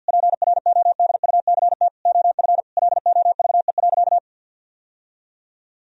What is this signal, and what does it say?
Morse code, heard as a keyed tone, about 50 wpm, 700 Hz. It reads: PRODUCT OF LOVE.
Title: Sentences from Top 1000 Words - Review Part 2 - 50wpm